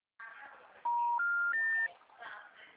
Special information tone.
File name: out- -2012- - - .414